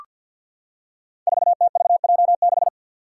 Morse code, heard as a keyed tone, about 53 wpm, 700 Hz. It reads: 4TVJ6
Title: Five Challenging Letter-Number Combinations 50wpm